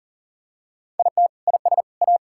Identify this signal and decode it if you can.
Morse code, heard as a keyed tone, about 37 wpm, 700 Hz. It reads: IT IS A
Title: Sentences from Top 1000 Words - Review Part 6 - 40wpm